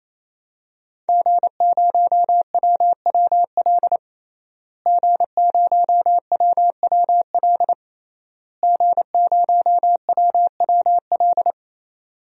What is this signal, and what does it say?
Morse code, heard as a keyed tone, about 28 wpm, 700 Hz. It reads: G0WWL G0WWL G0WWL
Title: DXCC-100 Call Signs - 5 Characters - Repeated 3x - 28wpm